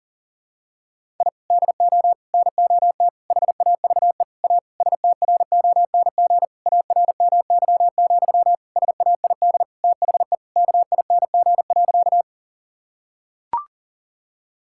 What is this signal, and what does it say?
Morse code, heard as a keyed tone, about 40 wpm, 700 Hz. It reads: I DO NOT HAVE A STRONG ARMY, SAID THE KING.
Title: Sentences from Top 1000 Words - Review Part 5 - 40wpm